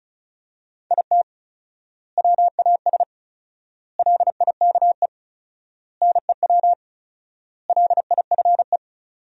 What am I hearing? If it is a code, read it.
Morse code, heard as a keyed tone, about 36 wpm, 700 Hz. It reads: IT WAS LIKE NEW LIFE
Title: 4x Word-Spacing - Sentences from Top 300 Words - 35wpm